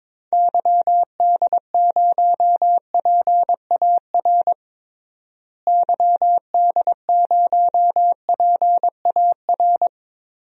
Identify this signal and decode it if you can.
Morse code, heard as a keyed tone, about 22 wpm, 700 Hz. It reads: YD0PAR YD0PAR
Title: DXCC-100 Call Signs - 6 Characters - Repeated 2x - 22wpm